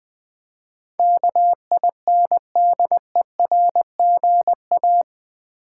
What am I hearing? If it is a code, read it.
Morse code, heard as a keyed tone, about 20 wpm, 700 Hz. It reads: KINDERGA
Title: Build A Word - 4 Letters At A Time - 12-Letter Words - 20wpm